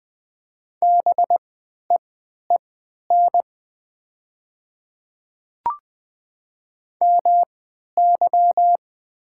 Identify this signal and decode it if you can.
Morse code, heard as a keyed tone, about 20 wpm, spaced as for 11 wpm, 700 Hz. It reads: BEEN MY
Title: 3x Character Spacing - Top 100 Words - 20wpm